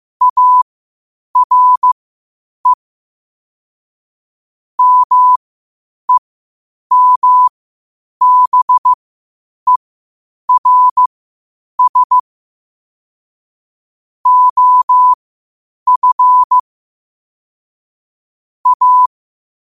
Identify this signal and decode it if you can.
Morse code, heard as a keyed tone, about 15 wpm, spaced as for 8 wpm, 1000 Hz. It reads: ARE MEMBERS OF A
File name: qotdfiles / 8wpm